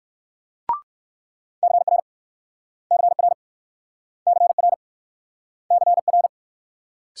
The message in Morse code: CR CR CR CR

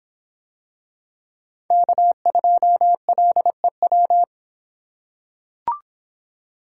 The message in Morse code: K2LEW

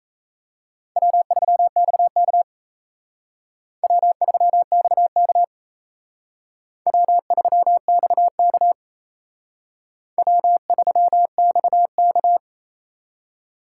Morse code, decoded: W3XK W3XK W3XK W3XK

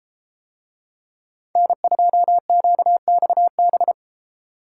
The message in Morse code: N2QXB

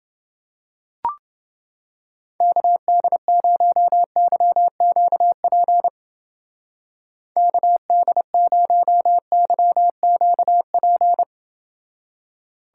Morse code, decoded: KD0YQP KD0YQP

